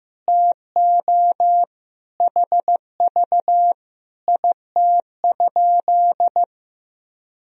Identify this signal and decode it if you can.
Morse code, heard as a keyed tone, about 15 wpm, 700 Hz. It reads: TO HV IT?
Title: Sentences from Top 100 Words and Abbreviations 15wpm